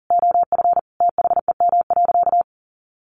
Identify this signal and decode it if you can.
Morse code, heard as a keyed tone, about 40 wpm, 700 Hz. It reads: OF THEM.
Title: Sentences from Top 1300 Words 40wpm